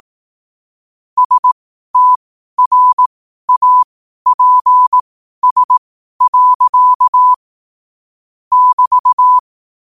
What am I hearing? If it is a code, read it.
Morse code, heard as a keyed tone, about 18 wpm, spaced as for 12 wpm, 1000 Hz. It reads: STRAPS. <BT>
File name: qotdfiles / 12wpm